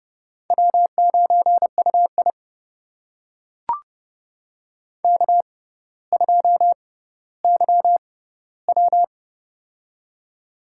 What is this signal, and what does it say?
Morse code, heard as a keyed tone, about 28 wpm, spaced as for 15 wpm, 700 Hz. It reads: W9UI K2YW
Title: USA Call Signs - 4 Characters - 6x Character Spacing - 30wpm